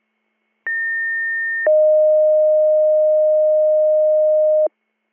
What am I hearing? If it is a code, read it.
Two-tone page, heard as a one-second tone, then a three-second tone, about 1800 then 620 Hz. It is heard from 0.7 to 4.7 s.